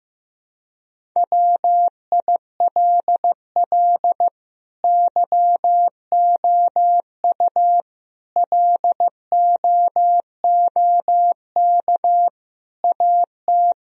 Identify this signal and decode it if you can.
Morse code, heard as a keyed tone, about 15 wpm, 700 Hz. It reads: WILL YOU LOOK AT
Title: Sentences from Top 1000 Words - Review Part 1 - 15wpm